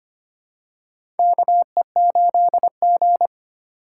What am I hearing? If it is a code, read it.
Morse code, heard as a keyed tone, about 25 wpm, 700 Hz. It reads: KE8G
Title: USA Call Signs - 4 Characters - 3x Character Spacing - 25wpm